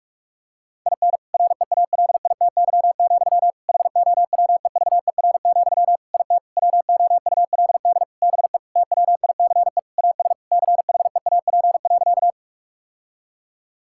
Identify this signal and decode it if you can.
Morse code, heard as a keyed tone, about 46 wpm, 700 Hz. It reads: IN REALITY, HOWEVER, IT WOULD BE TWICE AS CHEAP.
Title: Sentences from Top 1600 Words 45wpm